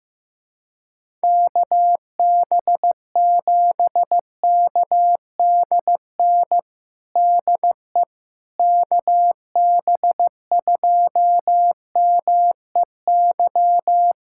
Morse code, heard as a keyed tone, about 16 wpm, 700 Hz. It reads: KB7KDN DE KB2MEY
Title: USA Callsign DE Callsign - 6 Characters - 15wpm